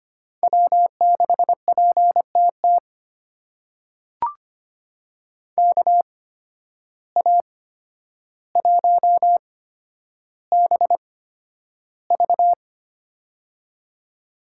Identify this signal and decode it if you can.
Morse code, heard as a keyed tone, about 25 wpm, spaced as for 11 wpm, 700 Hz. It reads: W6PTT KA1BV
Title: USA Call Signs - 5 Characters - 8x Character Spacing - 25wpm